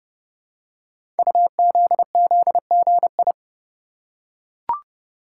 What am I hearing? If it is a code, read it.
Morse code, heard as a keyed tone, about 30 wpm, 700 Hz. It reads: UZZGI
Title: Code Groups - 5 Letters - 30wpm